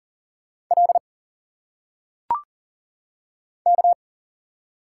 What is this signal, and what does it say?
Morse code, heard as a keyed tone, about 40 wpm, 700 Hz. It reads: L K
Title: Single Letters - Phonetic Pronunciation 40wpm